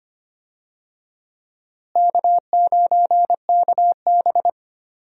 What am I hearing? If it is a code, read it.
Morse code, heard as a keyed tone, about 25 wpm, 700 Hz. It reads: K9KB
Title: USA Call Signs - 4 Characters - Repeated 3x - 25wpm